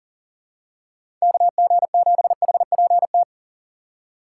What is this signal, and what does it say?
Morse code, heard as a keyed tone, about 40 wpm, 700 Hz. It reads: KG7HPT
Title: USA Call Signs - 6 Characters - 6x Character Spacing - 40wpm